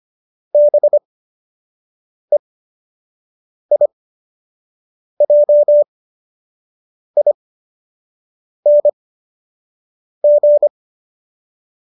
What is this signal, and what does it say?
Morse code, heard as a keyed tone, about 25 wpm, spaced as for 6 wpm, 590 Hz. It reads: BEIJING
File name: qso002 06